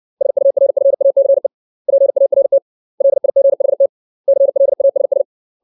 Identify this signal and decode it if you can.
Morse code, heard as a keyed tone, about 60 wpm, 550 Hz. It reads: SURFACE PART LIGHT CLASS
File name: 632 words / 632 Words 60WPM